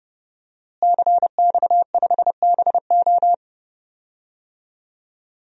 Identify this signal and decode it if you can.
Morse code, heard as a keyed tone, about 30 wpm, 700 Hz. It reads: CX5BO
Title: DXCC-100 Call Signs - 5 Characters - Repeated 3x - 30wpm